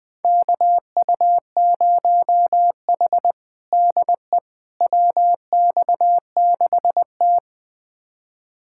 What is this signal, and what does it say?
Morse code, heard as a keyed tone, about 20 wpm, 700 Hz. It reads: KU0H DE WX6T
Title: USA Callsign DE Callsign - 4 Characters - 20wpm